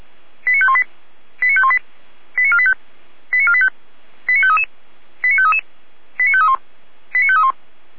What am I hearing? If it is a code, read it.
Five-tone selective call (ZVEI), heard as quick runs of five tones, tones of about 70 ms. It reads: 89518 89518 89485 89485 89533 89533 89531 89531